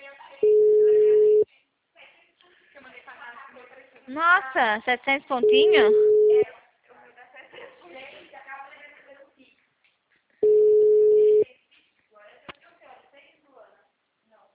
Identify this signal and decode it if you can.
Ringback tone (Europe, 425 Hz), with speech.